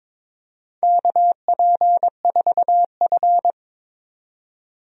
Morse code, heard as a keyed tone, about 22 wpm, 700 Hz. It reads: KP4F
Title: DXCC-100 Call Signs - 4 Characters - 22wpm